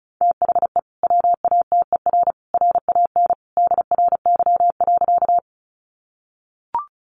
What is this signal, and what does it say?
Morse code, heard as a keyed tone, about 34 wpm, 700 Hz. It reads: THE WATER RAN DRY.
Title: Sentences from Top 500 Words 35wpm